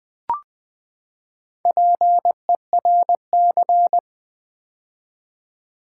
Morse code, heard as a keyed tone, about 20 wpm, 700 Hz. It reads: PERC
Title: Build A Word - 4 Letters At A Time - 10-Letter Words - 20wpm